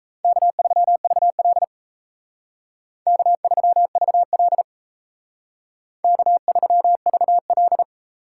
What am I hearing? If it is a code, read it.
Morse code, heard as a keyed tone, about 36 wpm, 700 Hz. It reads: K3VL K3VL K3VL